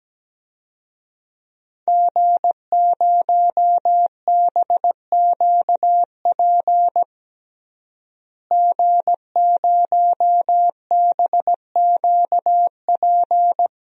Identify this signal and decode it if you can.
Morse code, heard as a keyed tone, about 17 wpm, 700 Hz. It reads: G0BQP G0BQP
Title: DXCC-100 Call Signs - 5 Characters - Repeated 3x - 17wpm